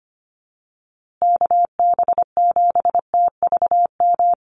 Morse code, heard as a keyed tone, about 25 wpm, 700 Hz. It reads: KB7TVM
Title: USA Call Signs Repeated 3x 25wpm